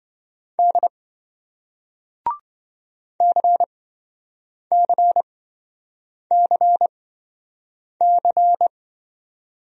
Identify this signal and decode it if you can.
Morse code, heard as a keyed tone, about 27 wpm, spaced as for 7 wpm, 700 Hz. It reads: D CCCC